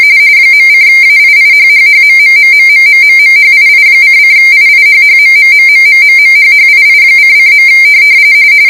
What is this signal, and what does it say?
RTTY, about 45.45 baud. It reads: .:AIIHHHMMMMMMMMHII::IIHHMMMMMMMHHII::.